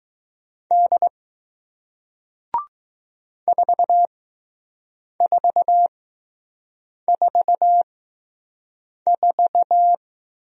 Morse code, heard as a keyed tone, about 20 wpm, spaced as for 6 wpm, 700 Hz. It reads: D 4444